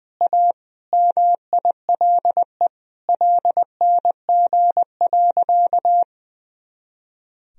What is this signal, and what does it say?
Morse code, heard as a keyed tone, about 20 wpm, 700 Hz. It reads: A MILE LNG.
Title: Sentences from Top 500 Words and Abbreviations 20wpm